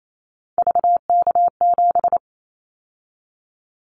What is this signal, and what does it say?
Morse code, heard as a keyed tone, about 28 wpm, 700 Hz. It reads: VK7
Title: Callsign Prefixes for Top 100 DXCC - 28wpm